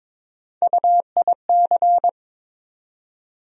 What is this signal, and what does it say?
Morse code, heard as a keyed tone, about 22 wpm, 700 Hz. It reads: UIC
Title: USA Call Sign Postfixes - 3 Letters 22wpm